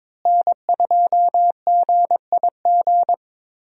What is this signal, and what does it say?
Morse code, heard as a keyed tone, about 22 wpm, 700 Hz. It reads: N2GIG
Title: USA Call Signs - 5 Characters - Repeated 3x - 22wpm